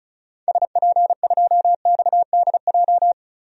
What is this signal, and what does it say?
Morse code, heard as a keyed tone, about 34 wpm, 700 Hz. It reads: SP2XDJ
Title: DXCC-100 Call Signs - 6 Characters - 35wpm